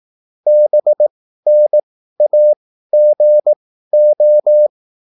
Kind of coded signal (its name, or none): Morse code